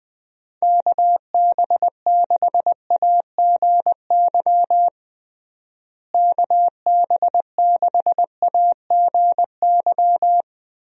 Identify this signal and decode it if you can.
Morse code, heard as a keyed tone, about 21 wpm, 700 Hz. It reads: KB6AGY KB6AGY
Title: USA Call Signs - 6 Characters - Repeated 3x - 20wpm